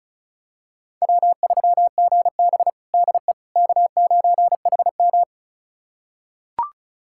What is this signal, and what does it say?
Morse code, heard as a keyed tone, about 36 wpm, 700 Hz. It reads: W3GB DE K9HM
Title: USA Callsign DE Callsign - 4 Characters - 35wpm